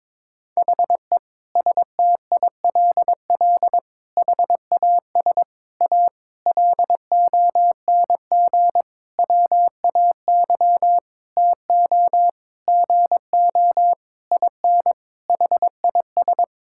Morse code, heard as a keyed tone, about 21 wpm, 700 Hz. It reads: HE STILL HAS A LONG WAY TO GO IN HIS
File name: Sentences with 7-Word Phrases - 22wpm